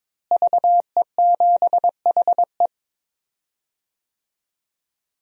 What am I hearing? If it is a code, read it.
Morse code, heard as a keyed tone, about 22 wpm, 700 Hz. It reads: VE7HE
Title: DXCC-100 Call Signs - 5 Characters - Repeated 2x - 22wpm